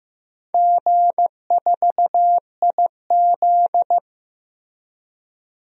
Morse code, heard as a keyed tone, about 15 wpm, 700 Hz. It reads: G4IZ
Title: DXCC-100 Call Signs - 4 Characters - 15wpm